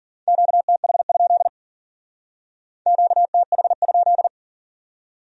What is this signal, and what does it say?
Morse code, heard as a keyed tone, about 53 wpm, 700 Hz. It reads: QTH? QTH?